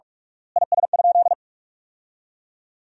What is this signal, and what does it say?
Morse code, heard as a keyed tone, about 46 wpm, 690 Hz. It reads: IS?
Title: 3x Word-Spacing - Sentences from Top 100 Words - 45wpm